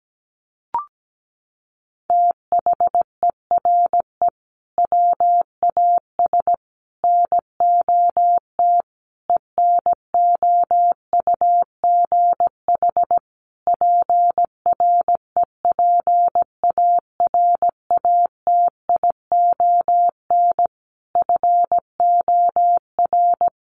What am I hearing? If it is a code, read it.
Morse code, heard as a keyed tone, about 17 wpm, 700 Hz. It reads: THERE WAS NOT ENOUGH PREPARATION FOR
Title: Sentences from Top 1400 Words 17wpm